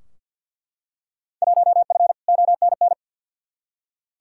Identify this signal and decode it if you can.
Morse code, heard as a keyed tone, about 49 wpm, 700 Hz. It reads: 1F ONN
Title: Field Day Exchanges Encore - 50wpm